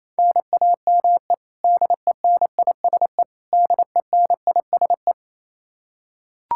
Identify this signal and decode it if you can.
Morse code, heard as a keyed tone, about 28 wpm, 700 Hz. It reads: NAME DENISE DENISE
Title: Practice QSO - Exchange 1 - Beginning - 28wpm